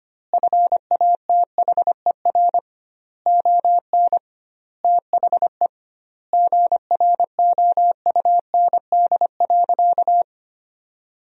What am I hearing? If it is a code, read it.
Morse code, heard as a keyed tone, about 25 wpm, 700 Hz. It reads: FATHER ON THE GROUND.
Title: 2x Word-Spacing - Sentences from Top 500 Words - 25wpm